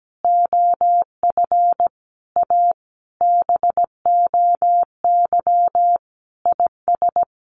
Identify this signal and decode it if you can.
Morse code, heard as a keyed tone, about 17 wpm, 700 Hz. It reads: OF A BOY IS
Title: Sentences from Top 1300 Words 17wpm